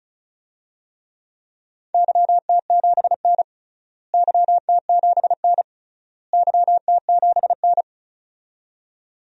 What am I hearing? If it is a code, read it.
Morse code, heard as a keyed tone, about 36 wpm, 700 Hz. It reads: YT7N YT7N YT7N
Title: DXCC-100 Call Signs - 4 Characters - Repeated 3x - 35wpm